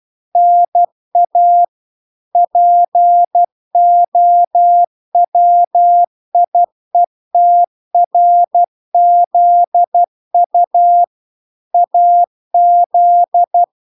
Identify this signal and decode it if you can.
Morse code, heard as a keyed tone, about 12 wpm, 700 Hz. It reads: NA POWIETRZU AZ
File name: recordings / 12wpm0000